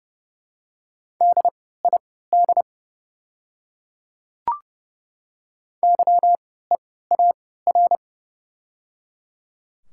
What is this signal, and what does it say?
Morse code, heard as a keyed tone, about 30 wpm, spaced as for 17 wpm, 700 Hz. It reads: DID YEAR
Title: 3x Character Spacing - Top 500 Words - 30wpm